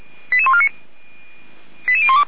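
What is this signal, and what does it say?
Five-tone selective call (ZVEI), heard as quick runs of five tones, tones of about 80 ms. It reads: 88149 88012
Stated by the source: ZVEI five-tone selective call received on an SDR, fire brigade Orlamünde (feuerwehr-orlamuende).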